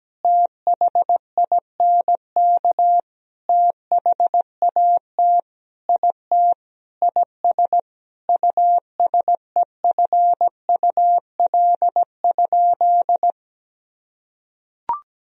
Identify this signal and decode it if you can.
Morse code, heard as a keyed tone, about 17 wpm, 700 Hz. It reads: THINK THAT IT IS USEFUL?
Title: Sentences from Top 1000 Words - Review Part 4 - 17wpm